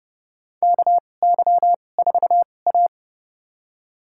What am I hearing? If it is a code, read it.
Morse code, heard as a keyed tone, about 30 wpm, spaced as for 22 wpm, 700 Hz. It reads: KY4A
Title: USA Call Signs - 4 Characters - 2x Character Spacing - 30wpm